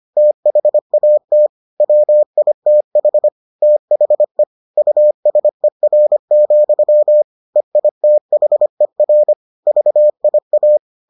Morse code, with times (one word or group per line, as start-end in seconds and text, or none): 0.2-1.5 THAT
1.8-3.3 WITH
3.6-4.4 THE
4.8-7.2 USER,
7.6-9.3 EITHER
9.7-10.8 VIA